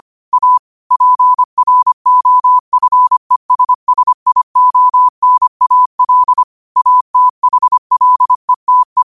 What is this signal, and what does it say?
Morse code, heard as a keyed tone, about 25 wpm, 1000 Hz. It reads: A PROFESSIONAL ATHLETE
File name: qotdfiles / 25wpm